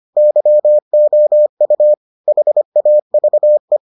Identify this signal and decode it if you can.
Morse code, heard as a keyed tone, about 25 wpm, 600 Hz. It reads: YOU HAVE